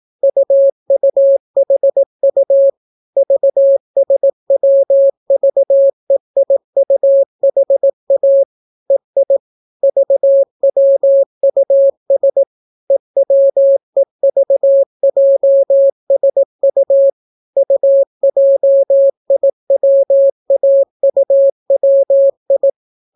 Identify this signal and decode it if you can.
Morse code, heard as a keyed tone, about 18 wpm, 550 Hz. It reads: UUHU VSWVEIUHA EI VWUS EWEVJSU UJIWAUWI